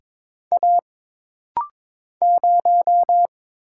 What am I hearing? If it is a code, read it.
Morse code, heard as a keyed tone, about 22 wpm, 700 Hz. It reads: A 0
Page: This rapid-fire timing is geared to push your ICR (Instant Character Recognition) skills.